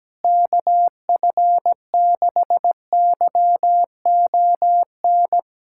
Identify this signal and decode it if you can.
Morse code, heard as a keyed tone, about 17 wpm, 700 Hz. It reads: KF6YON
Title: USA Callsign DE Callsign 17wpm